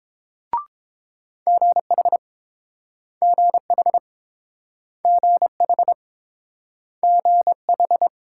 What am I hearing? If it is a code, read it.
Morse code, heard as a keyed tone, about 30 wpm, 700 Hz. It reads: GH GH GH GH